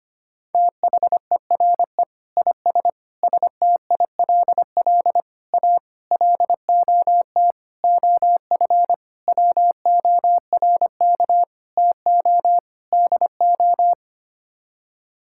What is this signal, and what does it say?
Morse code, heard as a keyed tone, about 25 wpm, 700 Hz. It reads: THERE IS STILL A LOT OF WORK TO DO
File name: Sets of 9 Words - 25wpm